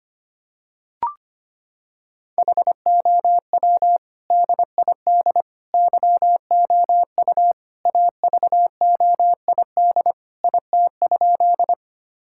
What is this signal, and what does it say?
Morse code, heard as a keyed tone, about 25 wpm, 700 Hz. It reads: HOW DID YOU AVOID IT?
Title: Sentences from Top 1000 Words - Review Part 6 - 25wpm